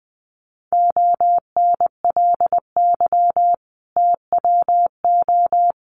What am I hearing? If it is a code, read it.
Morse code, heard as a keyed tone, about 20 wpm, 700 Hz. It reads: ONLY TWO
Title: Sentences from Top 200 Words 20wpm